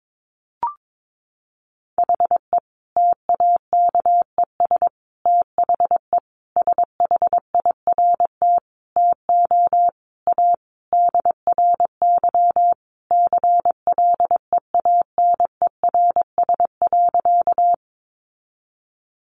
Morse code, heard as a keyed tone, about 22 wpm, 700 Hz. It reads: HE TAKES THE SHIRT TO A DRY CLEANERS.
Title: Additional Easy Phrases 22wpm